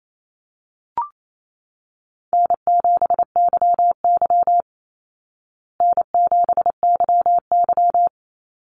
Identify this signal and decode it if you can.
Morse code, heard as a keyed tone, about 28 wpm, 700 Hz. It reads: N7YY N7YY
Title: Easy Call Signs Repeated 2x - 28wpm